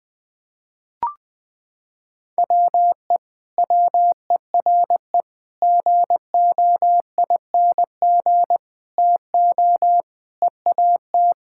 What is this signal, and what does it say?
Morse code, heard as a keyed tone, about 20 wpm, 700 Hz. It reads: WE WERE GOING TO EAT
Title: Sentences with 4-Word Phrases - 20wpm